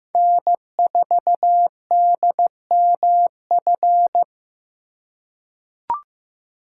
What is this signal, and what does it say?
Morse code, heard as a keyed tone, about 15 wpm, 700 Hz. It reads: N4DMF
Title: USA Call Signs - 3x Character Spacing - 15wpm